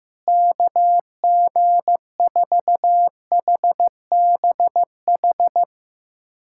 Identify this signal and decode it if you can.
Morse code, heard as a keyed tone, about 15 wpm, 700 Hz. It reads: KG4HBH
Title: USA Callsign DE Callsign 15wpm